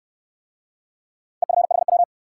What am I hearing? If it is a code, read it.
Morse code, heard as a keyed tone, about 62 wpm, 700 Hz. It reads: EFHW